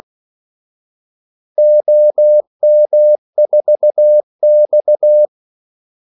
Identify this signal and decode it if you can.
Morse code, heard as a keyed tone, about 16 wpm, 600 Hz. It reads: OM4X
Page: Kaldesignaler 16 wpm